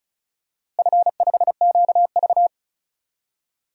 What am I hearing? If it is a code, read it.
Morse code, heard as a keyed tone, about 36 wpm, 700 Hz. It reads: F5QV
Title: DXCC-100 Call Signs - 4 Characters - Repeated 3x - 35wpm